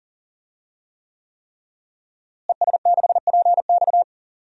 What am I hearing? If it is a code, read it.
Morse code, heard as a keyed tone, about 37 wpm, 700 Hz. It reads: ES6PX
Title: DXCC-100 Call Signs - 5 Characters - Repeated 3x - 40wpm